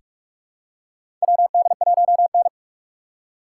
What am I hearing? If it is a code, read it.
Morse code, heard as a keyed tone, about 43 wpm, 700 Hz. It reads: WD1N
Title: USA Call Signs - 4 Characters - 6x Character Spacing - 45wpm